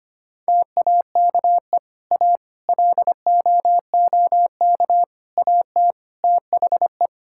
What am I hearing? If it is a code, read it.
Morse code, heard as a keyed tone, about 25 wpm, 700 Hz. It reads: TAKE A LOOK AT THE
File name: practice / 25wpm / Sentences with 4-Word Phrases - 25wpm